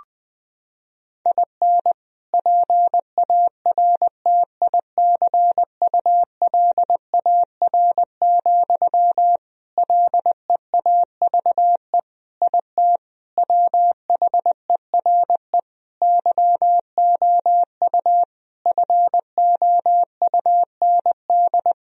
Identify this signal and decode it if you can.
Morse code, heard as a keyed tone, about 20 wpm, 700 Hz. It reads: IN PARTICULAR, LEAVE IT WHERE YOU FOUND
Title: Sentences from Top 1000 Words - Review Part 4 - 20wpm